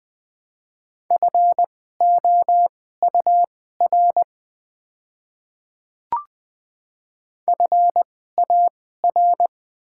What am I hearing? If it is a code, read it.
Morse code, heard as a keyed tone, about 20 wpm, spaced as for 15 wpm, 700 Hz. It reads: FOUR FAR
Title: 2x Character Spacing - Top 300 Words - 20wpm